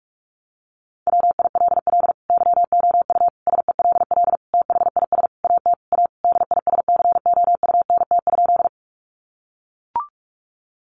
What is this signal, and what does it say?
Morse code, heard as a keyed tone, about 44 wpm, 700 Hz. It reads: WILL YOU SELL THIS AT A DISCOUNT?
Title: Sentences from Top 1200 Words 45wpm